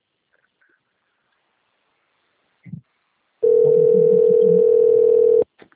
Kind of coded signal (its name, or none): telephone call progress tone